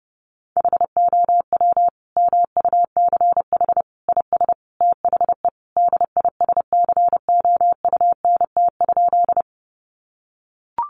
Morse code, heard as a keyed tone, about 30 wpm, 700 Hz. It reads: HOW MUCH IS THE DISCOUNT?
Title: Sentences from Top 1200 Words 30wpm